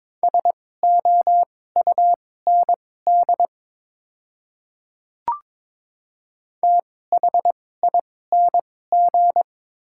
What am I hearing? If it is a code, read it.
Morse code, heard as a keyed tone, about 22 wpm, spaced as for 16 wpm, 700 Hz. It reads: SOUND THING
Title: 2x Character Spacing - Top 100 Words - 22wpm